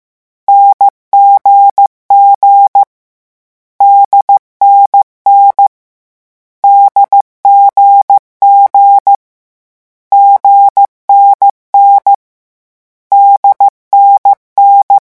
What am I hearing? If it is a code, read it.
Morse code, heard as a keyed tone, about 15 wpm, 800 Hz. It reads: NGG DNN DGG GNN DNN
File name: morse mp3 / lecon4-vitesse 14